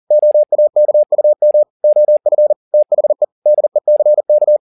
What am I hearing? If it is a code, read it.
Morse code, heard as a keyed tone, about 40 wpm, 600 Hz. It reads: OAKUM OF THE DECK